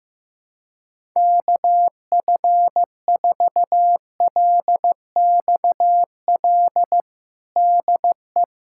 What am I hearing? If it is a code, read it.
Morse code, heard as a keyed tone, about 15 wpm, 700 Hz. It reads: KF4LXL DE
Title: USA Callsign DE Callsign - 6 Characters - 15wpm